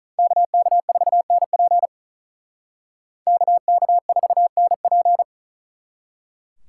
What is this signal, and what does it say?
Morse code, heard as a keyed tone, about 42 wpm, 700 Hz. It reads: KK4NP KK4NP
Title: Speed Racing - USA Call Signs 35wpm